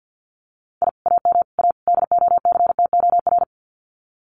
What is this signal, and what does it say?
Morse code, heard as a keyed tone, about 50 wpm, 700 Hz. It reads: I AM A DOCTOR
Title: Common QSO Phrases Encore 50wpm